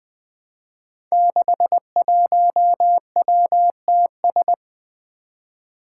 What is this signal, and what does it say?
Morse code, heard as a keyed tone, about 20 wpm, 700 Hz. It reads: 61WTS
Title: Five Challenging Letter-Number Combinations 20wpm